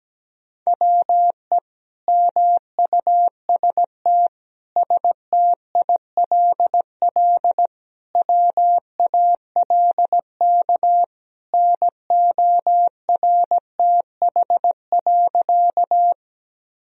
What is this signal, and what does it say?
Morse code, heard as a keyed tone, about 17 wpm, 700 Hz. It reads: WE MUST STILL WALK NORTH.
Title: Sentences from Top 1000 Words - Review Part 1 - 17wpm